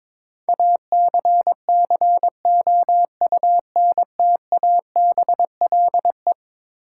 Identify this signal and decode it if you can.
Morse code, heard as a keyed tone, about 22 wpm, 700 Hz. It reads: ACCOUNTABLE
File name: practice / 22wpm / Eleven Letter Words - 22wpm